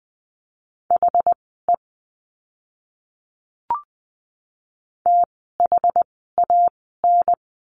Morse code, heard as a keyed tone, about 20 wpm, spaced as for 15 wpm, 700 Hz. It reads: HE THAN
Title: 185 Lesson 30f - Letter b - Words 20wpm